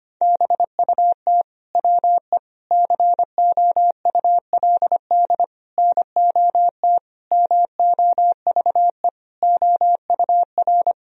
Morse code, heard as a keyed tone, about 25 wpm, 700 Hz. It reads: BUT WE COULD NOT MOVE OUR
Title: Sentences from Top 1000 Words - Review Part 2 - 25wpm